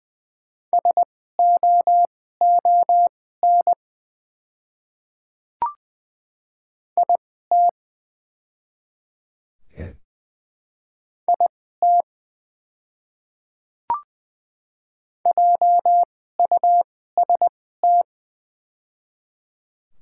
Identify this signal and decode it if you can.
Morse code, heard as a keyed tone, about 20 wpm, spaced as for 14 wpm, 700 Hz. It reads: SOON IT IT JUST
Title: 2x Character Spacing - Top 500 Words - 20wpm